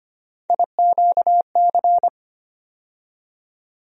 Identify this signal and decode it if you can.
Morse code, heard as a keyed tone, about 25 wpm, 700 Hz. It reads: IQC
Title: USA Call Sign Postfixes - 3 Letters 25wpm